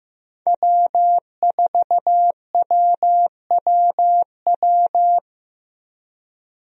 Morse code, heard as a keyed tone, about 15 wpm, 700 Hz. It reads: W4WWW
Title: USA Call Signs - 5 Characters - Repeated 2x - 15wpm